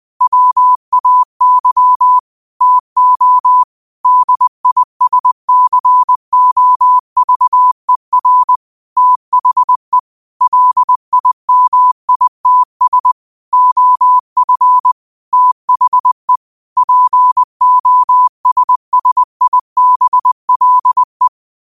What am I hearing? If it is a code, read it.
Morse code, heard as a keyed tone, about 20 wpm, 1000 Hz. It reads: WAY TO DISCOVER THE LIMITS OF THE POSSIBLE
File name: qotdfiles / 20wpm